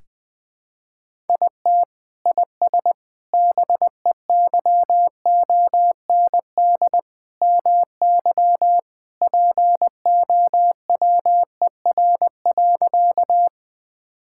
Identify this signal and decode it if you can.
Morse code, heard as a keyed tone, about 20 wpm, 700 Hz. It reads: IT IS BEYOND MY POWER.
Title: Sentences from Top 1000 Words - Review Part 6 - 20wpm